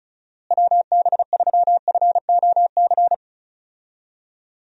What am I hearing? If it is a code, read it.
Morse code, heard as a keyed tone, about 36 wpm, 700 Hz. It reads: WB3FOC
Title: USA Call Signs - 6 Characters - Repeated 2x - 35wpm